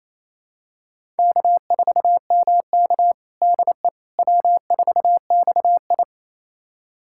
Morse code, heard as a keyed tone, about 28 wpm, 700 Hz. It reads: K4MK DE W4XI
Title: USA Callsign DE Callsign - 4 Characters - 28wpm